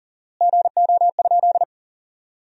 Morse code, heard as a keyed tone, about 40 wpm, 700 Hz. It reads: GO?
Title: 4x Word-Spacing - Sentences from Top 300 Words - 40wpm